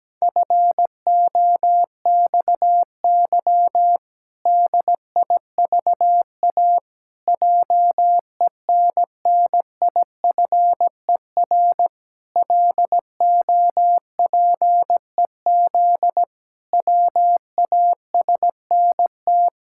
Morse code, heard as a keyed tone, about 17 wpm, 700 Hz. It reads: FOXY DIVA JENNIFER LOPEZ WASNT
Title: Pangram Challenge v2 - 17wpm